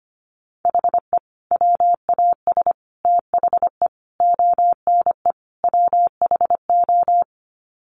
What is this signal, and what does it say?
Morse code, heard as a keyed tone, about 25 wpm, 700 Hz. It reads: HE WAS THE ONE WHO
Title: Sets of 5 Words Encore2 25wpm